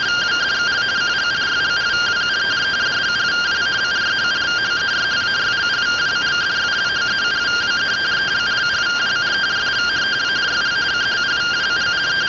Two-tone teleprinter signal, about 170 Hz shift.